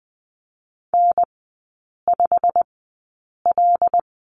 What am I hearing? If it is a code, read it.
Morse code, heard as a keyed tone, about 20 wpm, spaced as for 8 wpm, 700 Hz. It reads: N5L
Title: 087 Lesson 15e - Number 5 - Three Characters 20wpm